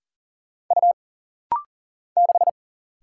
Morse code, heard as a keyed tone, about 40 wpm, 700 Hz. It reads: U 6
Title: Single Letter-Number - Rapid-Fire 40wpm